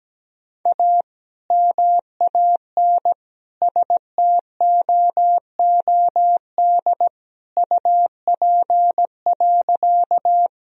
Morse code, heard as a keyed tone, about 17 wpm, 700 Hz. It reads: A MAN STOOD UP.
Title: Sentences from Top 1000 Words - Review Part 2 - 17wpm